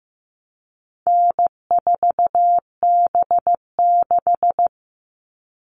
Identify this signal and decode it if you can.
Morse code, heard as a keyed tone, about 15 wpm, 700 Hz. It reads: N4B6
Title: All Cut Numbers - 4-Digit 15wpm